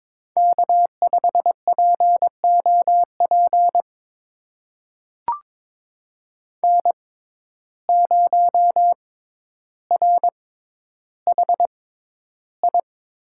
Morse code, heard as a keyed tone, about 22 wpm, spaced as for 11 wpm, 700 Hz. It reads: K5POP N0RHI
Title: USA Call Signs - 5 Characters - 6x Character Spacing - 22wpm